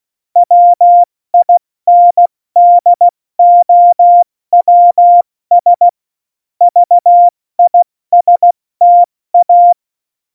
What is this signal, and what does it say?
Morse code, heard as a keyed tone, about 16 wpm, 700 Hz. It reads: WINDOWS VISTA
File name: Morsekurs / Witze / Witz 08. Januar 2019, 70 F80